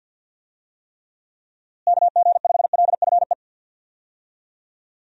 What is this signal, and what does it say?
Morse code, heard as a keyed tone, about 50 wpm, 700 Hz. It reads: KG5LFE
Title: USA Call Signs - 6 Characters - Repeated 3x - 50wpm